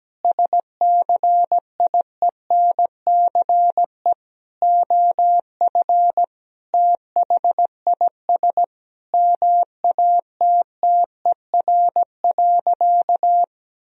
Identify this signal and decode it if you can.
Morse code, heard as a keyed tone, about 17 wpm, 700 Hz. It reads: SCIENCE OF THIS MATTER.
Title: Sentences from Top 1000 Words - Review Part 2 - 17wpm